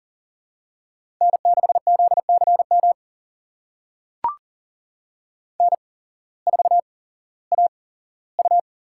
Morse code, heard as a keyed tone, about 40 wpm, spaced as for 19 wpm, 700 Hz. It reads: N6ZCM N4AU
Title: USA Call Signs - 5 Characters - 8x Character Spacing - 40wpm